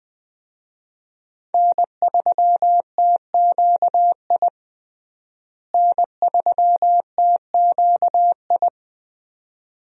Morse code, heard as a keyed tone, about 21 wpm, 700 Hz. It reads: N3TQI N3TQI
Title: USA Call Signs - 5 Characters - Repeated 2x - 20wpm